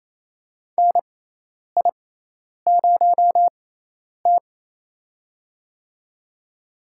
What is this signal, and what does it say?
Morse code, heard as a keyed tone, about 28 wpm, spaced as for 10 wpm, 700 Hz. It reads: NI0T
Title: USA Call Signs - 4 Characters - 6x Character Spacing - 28wpm